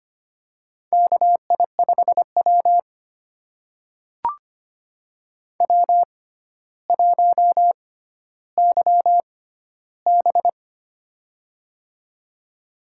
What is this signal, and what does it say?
Morse code, heard as a keyed tone, about 25 wpm, spaced as for 13 wpm, 700 Hz. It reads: KI5W W1YB
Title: USA Call Signs - 6x Character Spacing - 25wpm